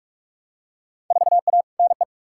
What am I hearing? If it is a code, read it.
Morse code, heard as a keyed tone, about 45 wpm, 700 Hz. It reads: 4A NE
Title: Field Day Exchanges - 45wpm